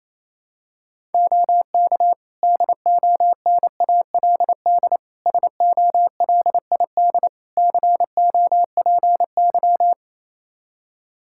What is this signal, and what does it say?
Morse code, heard as a keyed tone, about 28 wpm, 700 Hz. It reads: OK DONALD SOLID COPY
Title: Practice QSO - Exchange 3 - More Info - 28wpm